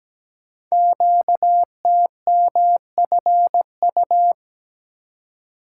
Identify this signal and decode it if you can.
Morse code, heard as a keyed tone, about 17 wpm, 700 Hz. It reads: QTMFU
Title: Code Groups - 5 Letters - 17wpm